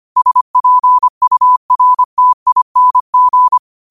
Morse code, heard as a keyed tone, about 25 wpm, 1000 Hz. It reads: SPURTING
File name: 25wpm